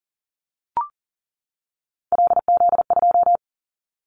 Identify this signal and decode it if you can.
Morse code, heard as a keyed tone, about 41 wpm, 700 Hz. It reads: LZ2
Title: Callsign Prefixes for Top 100 DXCC - 40wpm